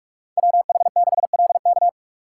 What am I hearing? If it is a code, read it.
Morse code, heard as a keyed tone, about 44 wpm, 700 Hz. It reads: WH6LK